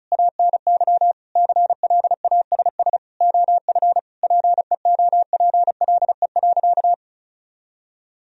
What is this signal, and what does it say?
Morse code, heard as a keyed tone, about 34 wpm, 700 Hz. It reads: ANY CLASS OF PEOPLE.
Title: Sentences from Top 1000 Words - Review Part 2 - 35wpm